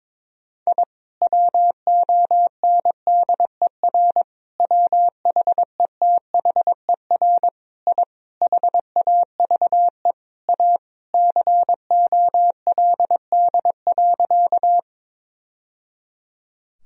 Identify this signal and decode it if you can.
Morse code, heard as a keyed tone, about 22 wpm, 700 Hz. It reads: I WONDER WHETHER I HAVE A COLD.
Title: Sentences from Top 1000 Words - Review Part 3 - 22wpm